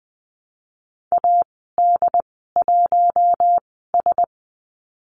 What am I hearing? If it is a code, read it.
Morse code, heard as a keyed tone, about 20 wpm, 700 Hz. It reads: A D 1 S
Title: 076 Lesson 13g - Letter L - Call Signs 20wpm